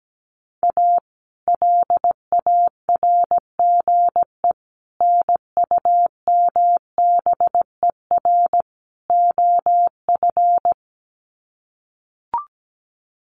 Morse code, heard as a keyed tone, about 17 wpm, 700 Hz. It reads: A LARGE NUMBER OF
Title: Sets of 4 Words 17wpm